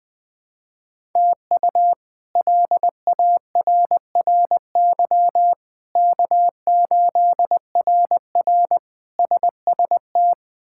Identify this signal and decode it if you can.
Morse code, heard as a keyed tone, about 20 wpm, 700 Hz. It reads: TU LARRY K8RR SST
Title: K1USN SST QSO Practice - 20wpm